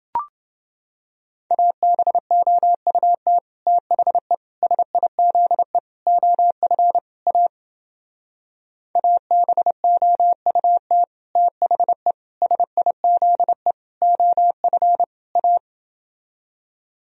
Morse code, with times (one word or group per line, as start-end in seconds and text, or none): 1.5-3.4 ABOUT
3.7-4.3 THE
4.6-5.8 SIZE
6.1-7.0 OF
7.3-7.5 A
9.0-11.0 ABOUT
11.4-12.1 THE
12.4-13.7 SIZE
14.0-15.0 OF
15.4-15.6 A